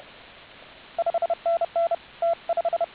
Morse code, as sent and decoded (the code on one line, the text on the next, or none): ..... -. -. / - .....
5NN T5